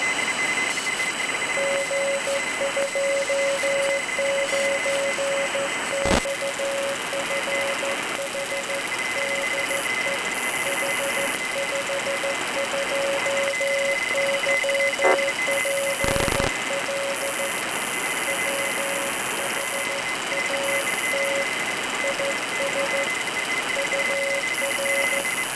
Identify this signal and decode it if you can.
Morse code, heard as a keyed tone, about 14 wpm, 550 Hz, with noise. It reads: G29XFH DE H52YPL WHAT IS UR